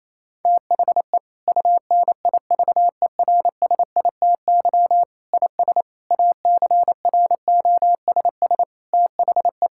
Morse code, heard as a keyed tone, about 27 wpm, 700 Hz. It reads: THE UNIVERSITY IS ACROSS THE
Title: Sentences from Top 1000 Words - Review Part 5 - 28wpm